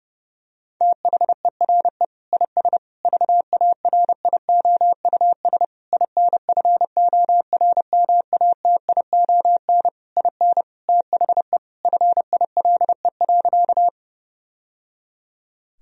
Morse code, heard as a keyed tone, about 30 wpm, 700 Hz. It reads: THERE IS VARIOUS INFORMATION IN THE FILE.
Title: Sentences from Top 1000 Words - Review Part 4 - 30wpm